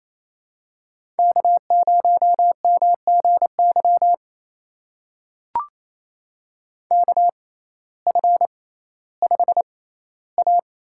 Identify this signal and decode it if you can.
Morse code, heard as a keyed tone, about 27 wpm, spaced as for 15 wpm, 700 Hz. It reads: K0MGY KF5A
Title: USA Call Signs - 5 Characters - 6x Character Spacing - 28wpm